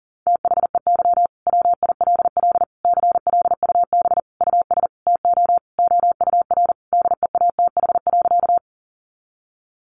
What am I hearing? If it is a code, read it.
Morse code, heard as a keyed tone, about 40 wpm, 700 Hz. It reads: THEY WILL CLUB US TO OUR DEATH.
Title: Sentences from Top 1200 Words 40wpm